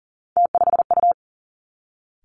Morse code, heard as a keyed tone, about 38 wpm, 700 Hz. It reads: T5U
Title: Three Challenging Letter-Number Combinations 40wpm